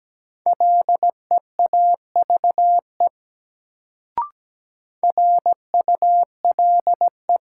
Morse code, heard as a keyed tone, about 17 wpm, 700 Hz. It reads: LEAVE RULE
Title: Sentences from Top 400 Words - New Word Drill - 17wpm